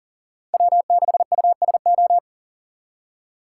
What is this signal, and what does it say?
Morse code, heard as a keyed tone, about 40 wpm, 700 Hz. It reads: W6USO
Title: USA Call Signs - 5 Characters - 4x Character Spacing - 40wpm